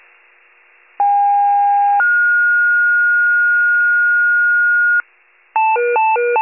Two-tone page, heard as a one-second tone, then a three-second tone, about 810 then 1430 Hz, with noise. It is heard from 1.0 to 5.0 s.
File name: wcalert